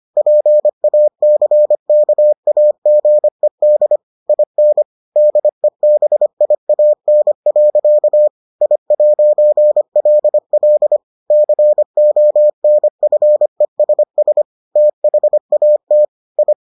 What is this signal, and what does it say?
Morse code, heard as a keyed tone, about 25 wpm, 600 Hz. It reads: PACKAGED IN DEBIAN. I'LL CONFESS THAT I